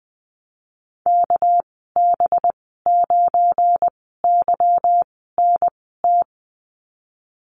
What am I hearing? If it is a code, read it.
Morse code, heard as a keyed tone, about 20 wpm, spaced as for 15 wpm, 700 Hz. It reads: KB9YNT
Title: 200 Lesson 32g - Letter k - Call Signs 20wpm